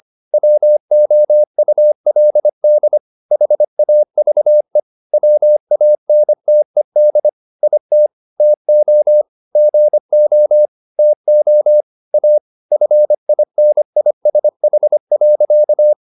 Morse code, heard as a keyed tone, about 25 wpm, 600 Hz. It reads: WOULD HAVE WANTED IT TO GO TO A FINISH.